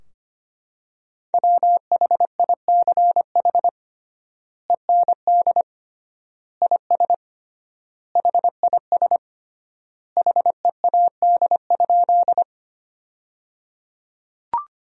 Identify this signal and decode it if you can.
Morse code, heard as a keyed tone, about 25 wpm, 700 Hz. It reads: WHICH END IS HIS HEAD?
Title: 3x Word-Spacing - Sentences from Top 300 Words - 25wpm